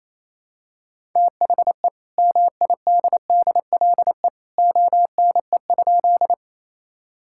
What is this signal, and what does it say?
Morse code, heard as a keyed tone, about 28 wpm, 700 Hz. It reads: THE MIDDLE ONE?
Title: Sentences from Top 1000 Words - Review Part 6 - 28wpm